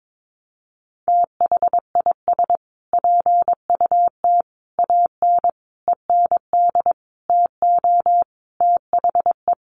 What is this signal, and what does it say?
Morse code, heard as a keyed tone, about 22 wpm, 700 Hz. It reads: THIS PUT AN END TO THE
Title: Sentences from Top 1100 Words 22wpm